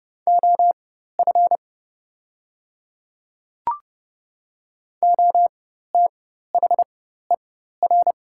Morse code, heard as a keyed tone, about 29 wpm, spaced as for 14 wpm, 700 Hz. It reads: OF OTHER
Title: 4x Character Spacing - Top 100 Words - 30wpm